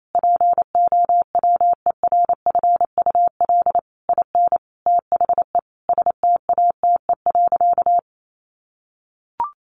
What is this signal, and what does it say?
Morse code, heard as a keyed tone, about 28 wpm, 700 Hz. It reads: POWERFUL IN THE STATE.
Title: Sentences from Top 1300 Words 28wpm